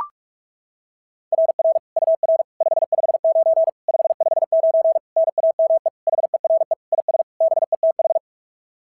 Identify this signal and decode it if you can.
Morse code, heard as a keyed tone, about 44 wpm, 660 Hz. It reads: RR UR 559 559 NAME HERE IS BETH
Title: SOTA QSO Practice 45wpm